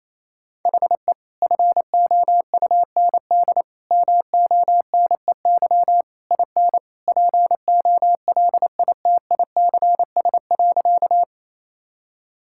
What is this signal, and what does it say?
Morse code, heard as a keyed tone, about 28 wpm, 700 Hz. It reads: HE FOUND MONEY IN POLITICS.
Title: Sentences from Top 1000 Words - Review Part 6 - 28wpm